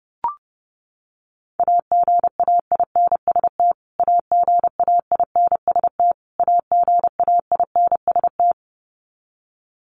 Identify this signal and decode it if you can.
Morse code, heard as a keyed tone, about 30 wpm, 700 Hz. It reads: AGAINST AGAINST AGAINST
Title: Top 500 Words Repeated 30wpm